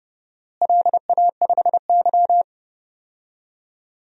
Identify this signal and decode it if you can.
Morse code, heard as a keyed tone, about 30 wpm, 700 Hz. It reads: LA5Y
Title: DXCC-100 Call Signs - 4 Characters - Repeated 2x - 30wpm